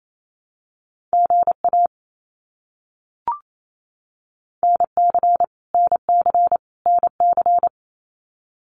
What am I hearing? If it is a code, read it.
Morse code, heard as a keyed tone, about 28 wpm, 700 Hz. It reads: GA NC NC NC
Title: US States Repeated 3x 28wpm